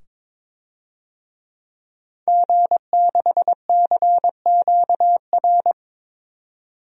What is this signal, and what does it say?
Morse code, heard as a keyed tone, about 22 wpm, 700 Hz. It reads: G6CQR
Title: DXCC-100 Call Signs - 5 Characters - 22wpm